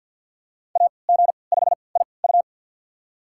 Morse code, heard as a keyed tone, about 46 wpm, spaced as for 27 wpm, 700 Hz. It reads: AG5IU
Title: USA Call Signs - 5 Characters - 3x Character Spacing - 50wpm